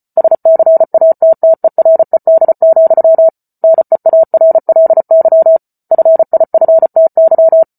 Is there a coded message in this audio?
Yes: Morse code from 0.2 to 7.6 s, heard as a keyed tone, about 34 wpm, 650 Hz. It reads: SCATTERED, NEARLY FIFTY